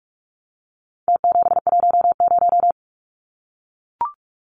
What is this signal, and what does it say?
Morse code, heard as a keyed tone, about 44 wpm, 700 Hz. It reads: T710
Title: Numbers - Contest Style - 4-Digit 45wpm